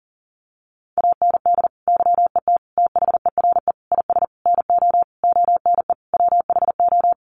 Morse code, heard as a keyed tone, about 40 wpm, 700 Hz. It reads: AND YET THERE IS NO ONE WHO
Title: Sentences from Top 1000 Words 40wpm